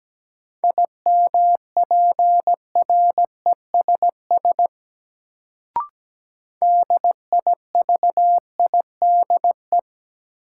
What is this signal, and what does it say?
Morse code, heard as a keyed tone, about 17 wpm, 700 Hz. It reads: IMPRESS DIVIDE
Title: Sentences from Top 1600 Words - New Word Drill - 17wpm